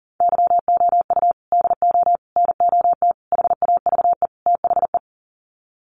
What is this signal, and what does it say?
Morse code, heard as a keyed tone, about 40 wpm, 700 Hz. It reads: YOU DO NOT HAVE THE
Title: Sets of 5 Words 40wpm